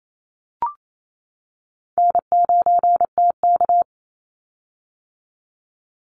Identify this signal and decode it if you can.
Morse code, heard as a keyed tone, about 28 wpm, 700 Hz. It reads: N9TK
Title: Easy Call Signs - 28wpm